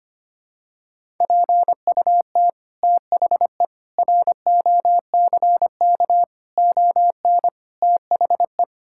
Morse code, heard as a keyed tone, about 25 wpm, 700 Hz. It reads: PUT THE ROCK ON THE
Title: Sentences from Top 1000 Words - Review Part 6 - 25wpm